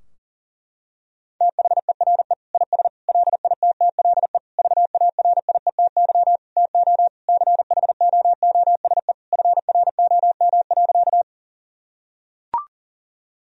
Morse code, heard as a keyed tone, about 40 wpm, 700 Hz. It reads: THERE IS LITTLE VARIETY TO CHOOSE FROM.
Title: Sentences from Top 1000 Words - Review Part 4 - 40wpm